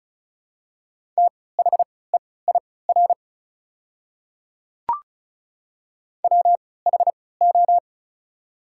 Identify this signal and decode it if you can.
Morse code, heard as a keyed tone, about 36 wpm, spaced as for 20 wpm, 690 Hz. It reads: THEIR WHO
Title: 3x Character Spacing - Top 100 Words - 35wpm